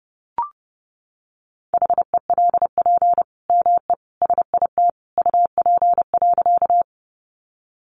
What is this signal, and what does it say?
Morse code, heard as a keyed tone, about 30 wpm, 700 Hz. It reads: HELP ME SIT UP.
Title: Sentences from Top 500 Words 30wpm